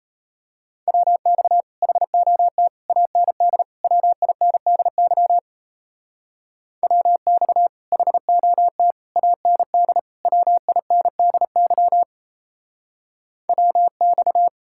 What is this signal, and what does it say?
Morse code, heard as a keyed tone, about 32 wpm, 700 Hz. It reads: WX HOT AND WINDY WX HOT AND WINDY WX
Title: Speed-Racing Common QSO Phrases - 28wpm